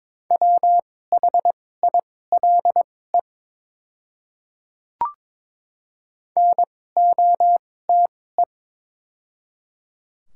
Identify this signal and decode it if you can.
Morse code, heard as a keyed tone, about 22 wpm, spaced as for 16 wpm, 700 Hz. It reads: WHILE NOTE
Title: 2x Character Spacing - Top 500 Words - 22wpm